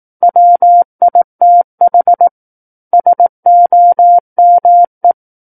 Morse code, heard as a keyed tone, about 18 wpm, 700 Hz. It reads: WITH SOME
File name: Book One / 09-Chapter 9 - 18 WPM